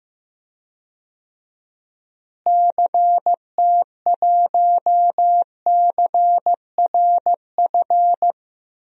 Morse code, heard as a keyed tone, about 15 wpm, 700 Hz. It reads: CT1CRF
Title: DXCC-100 Call Signs - 6 Characters - Repeated 2x - 15wpm